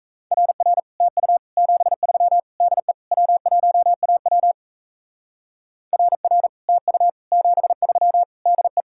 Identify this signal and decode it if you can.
Morse code, heard as a keyed tone, about 34 wpm, 700 Hz. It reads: RR TU 73 DE W1AW RR TU 73 DE
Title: Speed-Racing Common QSO Phrases - 28wpm